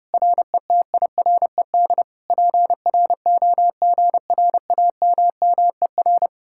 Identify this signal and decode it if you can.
Morse code, heard as a keyed tone, about 33 wpm, 700 Hz. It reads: RETIRED PROGRAMMER